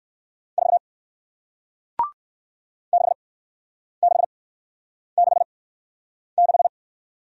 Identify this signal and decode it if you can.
Morse code, heard as a keyed tone, about 58 wpm, spaced as for 9 wpm, 700 Hz. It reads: 4 6666